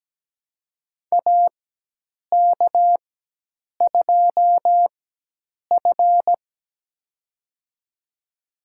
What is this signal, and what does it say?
Morse code, heard as a keyed tone, about 17 wpm, spaced as for 8 wpm, 700 Hz. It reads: AK2F
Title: USA Call Signs - 4 Characters - 4x Character Spacing - 17wpm